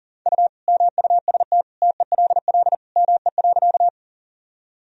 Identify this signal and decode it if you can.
Morse code, heard as a keyed tone, about 40 wpm, 700 Hz. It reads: U MUST TELL ME.
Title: Sentences from Top 200 Words and Abbreviations 40wpm